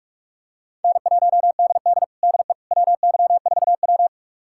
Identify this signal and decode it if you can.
Morse code, heard as a keyed tone, about 44 wpm, 700 Hz. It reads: N1DD DE WY4W
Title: USA Callsign DE Callsign - 4 Characters - 45wpm